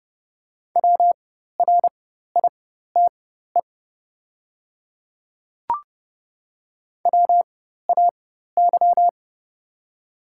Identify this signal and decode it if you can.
Morse code, heard as a keyed tone, about 30 wpm, spaced as for 14 wpm, 700 Hz. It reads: WRITE WAY
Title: 4x Character Spacing - Top 100 Words - 30wpm